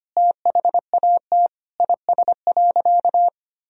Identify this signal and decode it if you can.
Morse code, heard as a keyed tone, about 25 wpm, 700 Hz. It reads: THAT IS.
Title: Sentences from Top 1000 Words - Review Part 4 - 25wpm